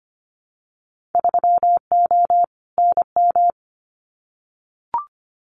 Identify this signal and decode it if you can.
Morse code, heard as a keyed tone, about 25 wpm, 700 Hz. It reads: 3O NM
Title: Winter Field Day Exchanges Encore 25wpm